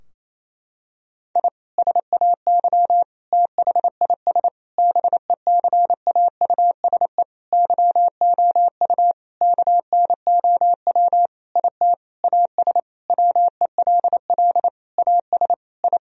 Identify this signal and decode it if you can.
Morse code, heard as a keyed tone, about 27 wpm, 700 Hz. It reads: I SAY THIS BECAUSE YOU KNOW IT AS WELL AS I
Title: Sentences from Top 1000 Words - Review Part 3 - 28wpm